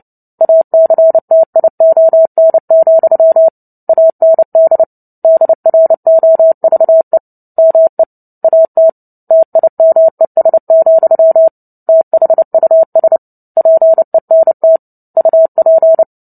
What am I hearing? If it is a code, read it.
Morse code, heard as a keyed tone, about 30 wpm, 650 Hz. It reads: ACTION, AND DROVE ME AT TIMES, THUS PENT UP
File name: Book Two / 03-Chapter 3 - 29 WPM